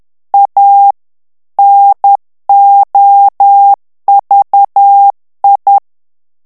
Morse code, heard as a keyed tone, about 10 wpm, 800 Hz. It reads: A NOVI
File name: morse mp3 / ITALIEN texte clair 1 vitesse 10 mots minute